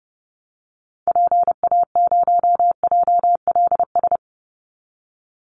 Morse code, heard as a keyed tone, about 29 wpm, 700 Hz. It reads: PA0JLS
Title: Call Signs Repeated 2x - 30wpm